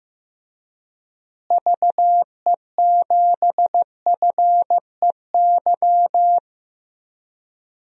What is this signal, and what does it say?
Morse code, heard as a keyed tone, about 15 wpm, 700 Hz. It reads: VE7FEY
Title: DXCC-100 Call Signs - 6 Characters - Repeated 3x - 15wpm